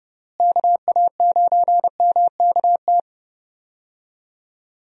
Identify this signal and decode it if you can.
Morse code, heard as a keyed tone, about 30 wpm, 700 Hz. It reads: KA9MKT
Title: USA Call Signs - 6 Characters - Repeated 2x - 30wpm